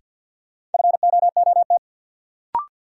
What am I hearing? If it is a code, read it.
Morse code, heard as a keyed tone, about 49 wpm, 700 Hz. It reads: FOOT
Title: Sentences from Top 500 Words - New Word Drill - 50wpm